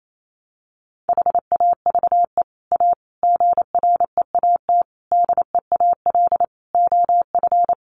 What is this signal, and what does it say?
Morse code, heard as a keyed tone, about 28 wpm, 700 Hz. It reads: HAVE A GREAT DEAL OF
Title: Sets of 5 Words Encore2 28wpm